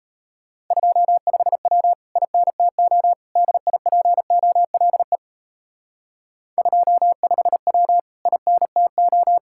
Morse code, heard as a keyed tone, about 35 wpm, 700 Hz. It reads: 25W INTO DIPOLE 25W INTO